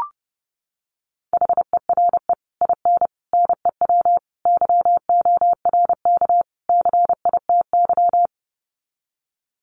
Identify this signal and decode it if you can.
Morse code, heard as a keyed tone, about 30 wpm, 700 Hz. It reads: HERE IN NEW YORK CITY
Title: Sets of 5 Words Encore2 30wpm